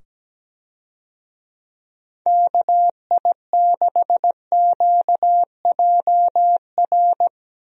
Morse code, heard as a keyed tone, about 17 wpm, 700 Hz. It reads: KI6QJR
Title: USA Call Signs - 6 Characters - 17wpm